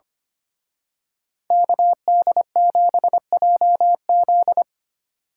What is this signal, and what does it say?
Morse code, heard as a keyed tone, about 29 wpm, 700 Hz. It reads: KD7JZ
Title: Speed Racing - USA Call Signs 25wpm